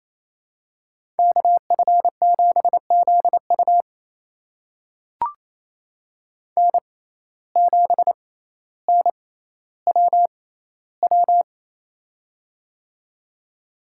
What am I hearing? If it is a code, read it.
Morse code, heard as a keyed tone, about 28 wpm, spaced as for 14 wpm, 700 Hz. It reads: KF7ZU N7NWW
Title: USA Call Signs - 5 Characters - 6x Character Spacing - 28wpm